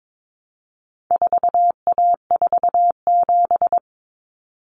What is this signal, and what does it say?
Morse code, heard as a keyed tone, about 22 wpm, 700 Hz. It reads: 4A47
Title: Common Cut Numbers - 4-Digit 22wpm